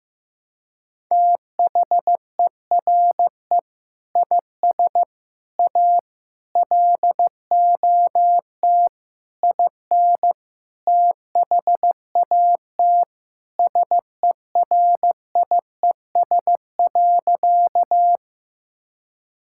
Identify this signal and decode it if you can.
Morse code, heard as a keyed tone, about 15 wpm, 700 Hz. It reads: THERE IS A LOT IN THAT SERIES.
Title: Sentences from Top 1000 Words - Review Part 5 - 15wpm